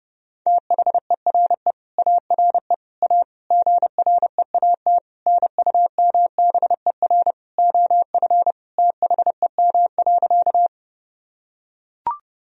Morse code, heard as a keyed tone, about 30 wpm, 700 Hz. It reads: THERE ARE A GREAT NUMBER OF THEM.
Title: Sentences from Top 1000 Words - Review Part 1 - 30wpm